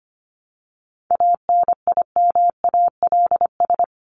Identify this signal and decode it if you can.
Morse code, heard as a keyed tone, about 25 wpm, 700 Hz. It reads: ANIMALS
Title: ES:S Suffix - Top 100 Words with Base Words 25wpm